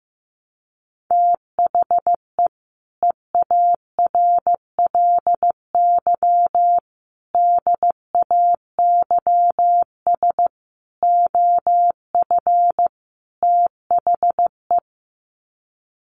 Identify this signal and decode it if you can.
Morse code, heard as a keyed tone, about 15 wpm, 700 Hz. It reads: THE EARLY DAYS OF THE
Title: Sets of 5 Words Encore 15wpm